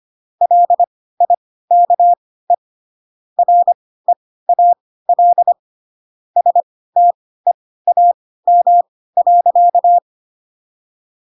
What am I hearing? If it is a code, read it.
Morse code, heard as a keyed tone, about 25 wpm, spaced as for 16 wpm, 700 Hz. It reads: LIKE REAL STEAM.
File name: Pinocchio / Pinocchio ch3 10 25 ik1hge